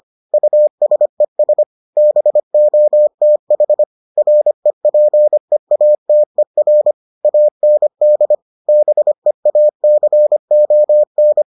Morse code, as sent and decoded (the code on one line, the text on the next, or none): ..- ... ... / -... --- - .... / .-. .--. .- - . .-. / .- -. -.. / -... .- -.-. --- -.
USES BOTH REPEATER AND BEACON